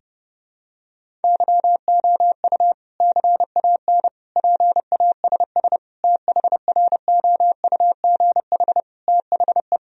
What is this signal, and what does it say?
Morse code, heard as a keyed tone, about 32 wpm, 700 Hz. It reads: YOU CAN PASS THROUGH THE
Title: Sentences from Top 1000 Words - Review Part 1 - 30wpm